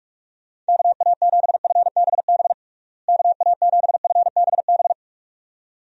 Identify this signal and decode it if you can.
Morse code, heard as a keyed tone, about 46 wpm, 700 Hz. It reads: KA7FBB KA7FBB
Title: USA Call Signs - 6 Characters - Repeated 2x - 45wpm